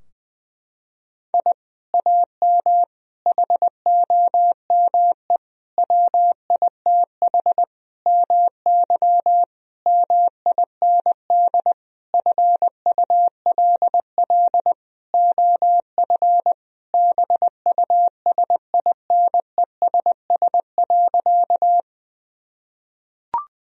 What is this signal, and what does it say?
Morse code, heard as a keyed tone, about 21 wpm, 700 Hz. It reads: I AM HOME WITH MY MIND FULL OF BUSINESS.
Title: Sentences from Top 1000 Words - Review Part 3 - 20wpm